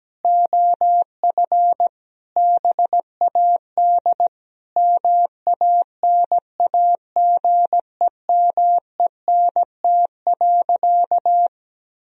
Morse code, heard as a keyed tone, about 17 wpm, 700 Hz. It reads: OF BAD MANAGEMENT.
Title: Sentences from Top 1000 Words - Review Part 4 - 17wpm